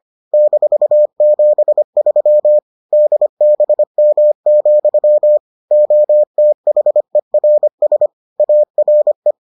Morse code, heard as a keyed tone, about 25 wpm, 600 Hz. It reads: -73 DBM, OTHERS ARE